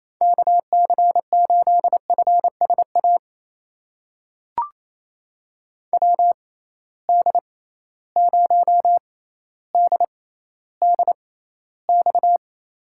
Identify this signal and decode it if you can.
Morse code, heard as a keyed tone, about 28 wpm, spaced as for 14 wpm, 700 Hz. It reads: KC8FSA WD0DDX
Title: USA Call Signs - 6 Characters - 6x Character Spacing - 28wpm